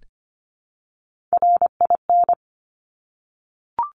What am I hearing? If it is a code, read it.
Morse code, heard as a keyed tone, about 25 wpm, 700 Hz. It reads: RIN
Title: Top Three Letter Combinations in Words 25wpm